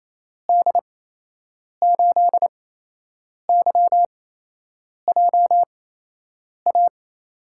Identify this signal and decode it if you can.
Morse code, heard as a keyed tone, about 28 wpm, spaced as for 8 wpm, 700 Hz. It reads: D8YJA
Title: USA Call Signs - 6 Characters - 8x Character Spacing - 28wpm